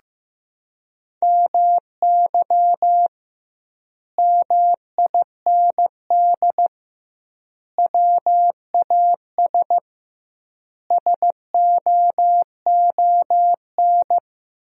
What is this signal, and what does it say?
Morse code, heard as a keyed tone, about 15 wpm, 700 Hz. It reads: MY MIND WAS SOON
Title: 2x Word-Spacing - Sentences from Top 500 Words - 15wpm